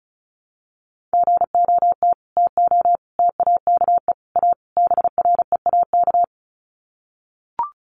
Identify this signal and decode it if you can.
Morse code, heard as a keyed tone, about 34 wpm, 700 Hz. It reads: GOT TO TAKE A BREAK
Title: Sets of 5 Words Encore 35wpm